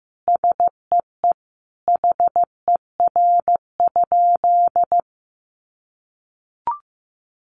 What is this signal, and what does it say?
Morse code, heard as a keyed tone, about 15 wpm, 700 Hz. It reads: SEE HER?
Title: Sentences from Top 100 Words 15wpm